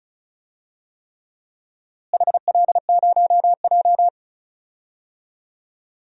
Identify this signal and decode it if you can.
Morse code, heard as a keyed tone, about 34 wpm, 700 Hz. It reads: HL0J
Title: DXCC-100 Call Signs - 4 Characters - Repeated 3x - 35wpm